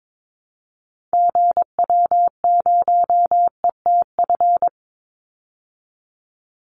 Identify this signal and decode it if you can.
Morse code, heard as a keyed tone, about 22 wpm, 700 Hz. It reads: GW0ETF
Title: Call Signs Repeated 2x - 22wpm